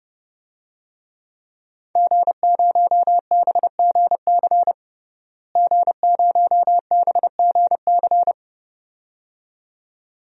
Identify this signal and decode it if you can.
Morse code, heard as a keyed tone, about 30 wpm, 700 Hz. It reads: G0BGC G0BGC
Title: DXCC-100 Call Signs - 5 Characters - Repeated 2x - 30wpm